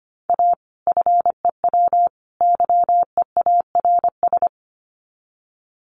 Morse code, heard as a keyed tone, about 25 wpm, 700 Hz. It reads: A FEW YEARS
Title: Sets of 3 Words 25wpm